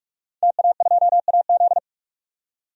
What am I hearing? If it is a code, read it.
Morse code, heard as a keyed tone, about 41 wpm, 700 Hz. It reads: TA2AZ
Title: DXCC-100 Call Signs - 5 Characters - 45wpm